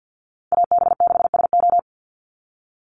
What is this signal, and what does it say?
Morse code, heard as a keyed tone, about 50 wpm, 700 Hz. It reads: AB6SO
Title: NA SOTA Chaser Callsigns 50wpm